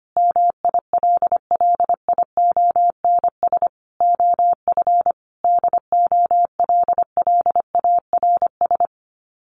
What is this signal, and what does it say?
Morse code, heard as a keyed tone, about 25 wpm, 700 Hz. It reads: MILLIONS OF DOLLARS
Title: Sets of 5 Words Encore 25wpm